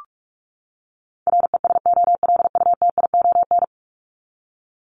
Morse code, heard as a keyed tone, about 44 wpm, 700 Hz. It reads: RESOLUTION
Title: Top 1501-1750 Words 45wpm